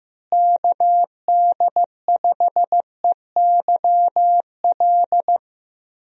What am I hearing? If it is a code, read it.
Morse code, heard as a keyed tone, about 16 wpm, 700 Hz. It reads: KD5EYL
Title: USA Call Signs - 6 Characters - Repeated 3x - 15wpm